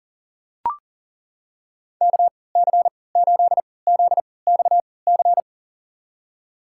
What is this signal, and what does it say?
Morse code, heard as a keyed tone, about 40 wpm, spaced as for 23 wpm, 700 Hz. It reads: KC8ZXC
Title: USA Call Signs - 6 Characters - 3x Character Spacing - 40wpm